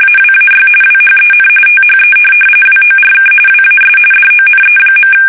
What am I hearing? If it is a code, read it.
RTTY, about 45.45 baud. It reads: AWAY WHEN THE WORK WAS DONE.
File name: fsk850 / benjamin-g-lamme